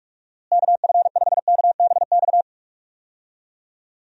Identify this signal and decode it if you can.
Morse code, heard as a keyed tone, about 44 wpm, 700 Hz. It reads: KF5KBX
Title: USA Call Signs - 6 Characters - Repeated 2x - 45wpm